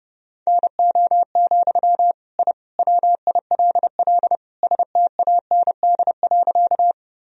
Morse code, heard as a keyed tone, about 30 wpm, 700 Hz. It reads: NO, I WILL STAND.
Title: Sentences from Top 1000 Words - Review Part 1 - 30wpm